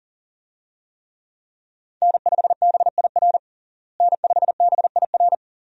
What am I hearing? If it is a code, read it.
Morse code, heard as a keyed tone, about 37 wpm, 700 Hz. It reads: N5BIR N5BIR
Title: USA Call Signs - 5 Characters - Repeated 3x - 40wpm